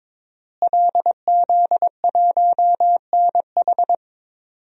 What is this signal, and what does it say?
Morse code, heard as a keyed tone, about 22 wpm, 700 Hz. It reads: LZ1NH
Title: DXCC-100 Call Signs - 5 Characters - Repeated 2x - 22wpm